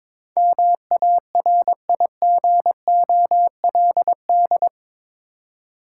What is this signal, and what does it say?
Morse code, heard as a keyed tone, about 22 wpm, 690 Hz. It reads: MARIGOLD
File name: Difficult Words with Hints - 22wpm